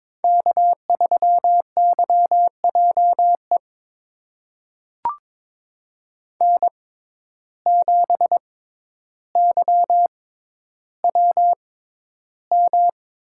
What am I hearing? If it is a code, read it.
Morse code, heard as a keyed tone, about 22 wpm, spaced as for 11 wpm, 700 Hz. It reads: K3YJE N7YWM
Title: USA Call Signs - 5 Characters - 6x Character Spacing - 22wpm